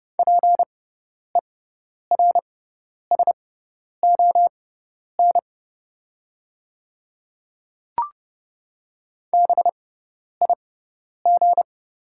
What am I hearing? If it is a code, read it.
Morse code, heard as a keyed tone, about 30 wpm, spaced as for 10 wpm, 700 Hz. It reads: PERSON BIG
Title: 6x Character Spacing - Top 500 Words - 30wpm